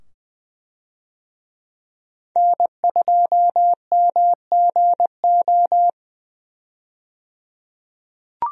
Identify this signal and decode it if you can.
Morse code, heard as a keyed tone, about 20 wpm, 700 Hz. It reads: N2MGO
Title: USA Call Signs - 5 Characters - 20wpm